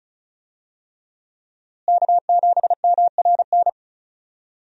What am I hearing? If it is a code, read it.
Morse code, heard as a keyed tone, about 34 wpm, 700 Hz. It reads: K7MRN
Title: USA Call Signs - 5 Characters - Repeated 3x - 35wpm